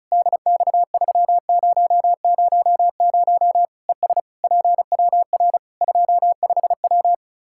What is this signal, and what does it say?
Morse code, heard as a keyed tone, about 36 wpm, 700 Hz. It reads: DX3000 ES PWR 25W
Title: Practice QSO - Exchange 2 - Info - 35wpm